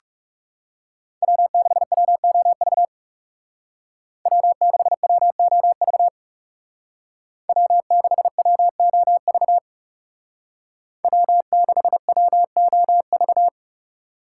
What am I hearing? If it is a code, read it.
Morse code, heard as a keyed tone, about 39 wpm, 700 Hz. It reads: W6WOV W6WOV W6WOV W6WOV